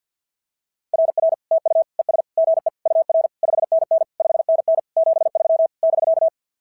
Morse code, heard as a keyed tone, about 46 wpm, 660 Hz. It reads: RR TU ES GE UR 5NN 5NN 73 <BK>
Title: SOTA QSO Practice 50wpm